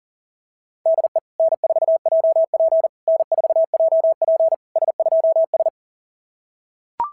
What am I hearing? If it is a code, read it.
Morse code, heard as a keyed tone, about 40 wpm, 690 Hz. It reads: DE N4JP N4JP S2S
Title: SOTA QSO Practice 40wpm